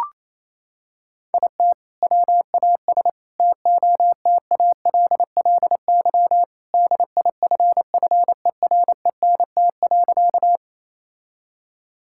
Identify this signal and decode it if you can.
Morse code, heard as a keyed tone, about 27 wpm, 700 Hz. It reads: IT WAS TOTALLY DIFFERENT.
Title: Sentences from Top 1600 Words 28wpm